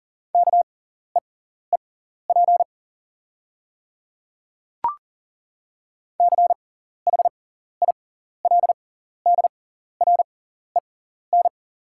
Morse code, heard as a keyed tone, about 39 wpm, spaced as for 14 wpm, 700 Hz. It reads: KEEP CHILDREN
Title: 6x Character Spacing - Top 300 Words - 40wpm